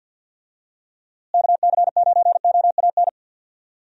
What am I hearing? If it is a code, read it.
Morse code, heard as a keyed tone, about 51 wpm, 700 Hz. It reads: KC9OAN
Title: USA Call Signs - 6 Characters - Repeated 3x - 50wpm